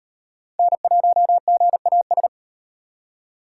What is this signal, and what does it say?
Morse code, heard as a keyed tone, about 37 wpm, 700 Hz. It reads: N1GAS